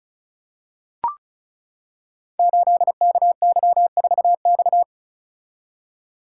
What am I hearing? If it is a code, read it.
Morse code, heard as a keyed tone, about 36 wpm, 700 Hz. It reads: 8KY4X
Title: Code Groups - 5 Letter:Numbers - 35wpm